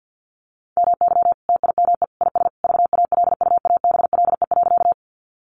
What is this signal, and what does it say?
Morse code, heard as a keyed tone, about 50 wpm, 700 Hz. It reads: MY TIME IS VALUABLE.
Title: Sentences from Top 1100 Words 50wpm